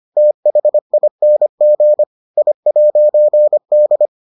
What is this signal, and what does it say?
Morse code, heard as a keyed tone, about 25 wpm, 600 Hz. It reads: THING I'D